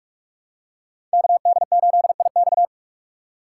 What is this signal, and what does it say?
Morse code, heard as a keyed tone, about 42 wpm, 700 Hz. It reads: KD8IX
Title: USA Call Signs - 6x Character Spacing - 45wpm